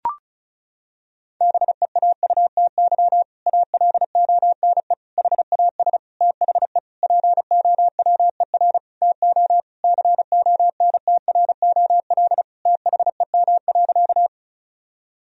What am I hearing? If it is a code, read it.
Morse code, heard as a keyed tone, about 34 wpm, 700 Hz. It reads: BEAUTY ALONE HAS THE POWER TO CONTROL THEM.
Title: Sentences from Top 1000 Words - Review Part 2 - 35wpm